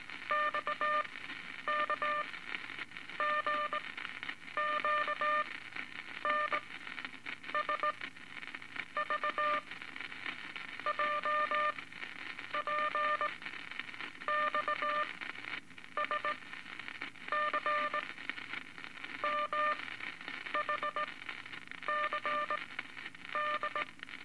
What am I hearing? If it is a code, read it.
Morse code, heard as a keyed tone, about 19 wpm, spaced as for 8 wpm, 1200 Hz, with noise. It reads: XKGQN SVJPX SCMHC D
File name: Morse Code Records / 17A LCM 162 Eight Words Per Minute